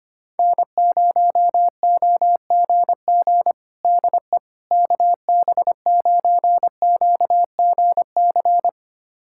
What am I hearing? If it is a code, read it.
Morse code, heard as a keyed tone, about 25 wpm, 700 Hz. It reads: N0OGG DE KB9QGC
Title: USA Callsign DE Callsign 25wpm